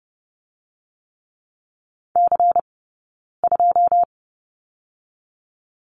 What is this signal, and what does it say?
Morse code, heard as a keyed tone, about 30 wpm, 700 Hz. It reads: C 2
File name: 092 Lesson 16d - Letter c - Two Characters 30wpm